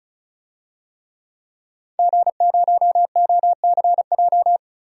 Morse code, heard as a keyed tone, about 34 wpm, 700 Hz. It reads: G0OCJ
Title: DXCC-100 Call Signs - 5 Characters - 35wpm